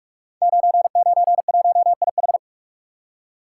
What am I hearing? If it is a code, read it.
Morse code, heard as a keyed tone, about 44 wpm, 690 Hz. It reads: 991IH
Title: Code Groups - 5 Letter:Numbers - 45wpm